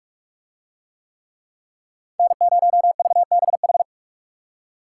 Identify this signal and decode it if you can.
Morse code, heard as a keyed tone, about 44 wpm, 700 Hz. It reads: N0VBH